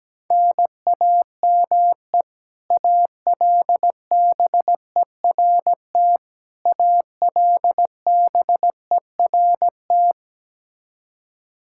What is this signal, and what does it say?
Morse code, heard as a keyed tone, about 17 wpm, 700 Hz. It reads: NAME ALBERT ALBERT
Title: Practice QSO - Exchange 1 - Beginning - 17wpm